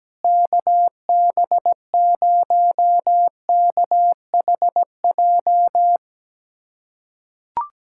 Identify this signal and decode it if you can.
Morse code, heard as a keyed tone, about 17 wpm, 700 Hz. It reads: KB0KHJ
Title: USA Callsign DE Callsign - 6 Characters - 17wpm